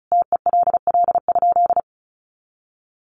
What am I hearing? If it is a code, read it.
Morse code, heard as a keyed tone, about 34 wpm, 700 Hz. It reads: TELL?
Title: Sentences from Top 200 Words 35wpm